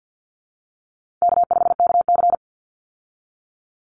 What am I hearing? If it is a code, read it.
Morse code, heard as a keyed tone, about 50 wpm, 700 Hz. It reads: K5KC
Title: Easy Call Signs Repeated 2x - 50wpm